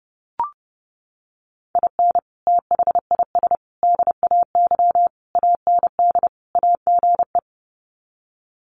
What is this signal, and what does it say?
Morse code, heard as a keyed tone, about 30 wpm, 700 Hz. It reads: IN THIS DAY AND AGE
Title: Sets of 5 Words Encore 30wpm